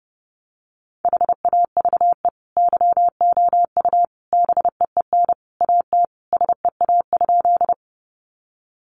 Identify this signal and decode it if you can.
Morse code, heard as a keyed tone, about 30 wpm, 700 Hz. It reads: HAVE YOU BEEN AT SEA?
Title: Sentences from Top 300 Words 30wpm